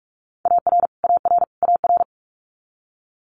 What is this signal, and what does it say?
Morse code, heard as a keyed tone, about 46 wpm, 700 Hz. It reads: AR AR AR
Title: Multi-letter Phonemes Repeated 3x 45wpm